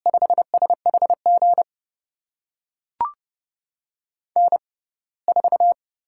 Morse code, heard as a keyed tone, about 30 wpm, spaced as for 21 wpm, 700 Hz. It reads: 5SHG N4
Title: USA Call Signs - 5 Characters - 6x Character Spacing - 30wpm